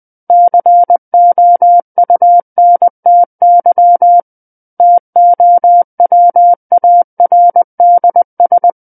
Morse code, heard as a keyed tone, about 20 wpm, 700 Hz. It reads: COUNTY TOWARDS
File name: 20 WPM